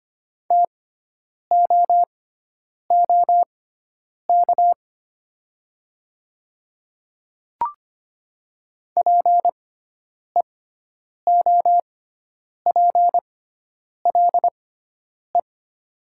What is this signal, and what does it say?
Morse code, heard as a keyed tone, about 25 wpm, spaced as for 9 wpm, 700 Hz. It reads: TOOK PEOPLE
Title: 6x Character Spacing - Top 300 Words - 25wpm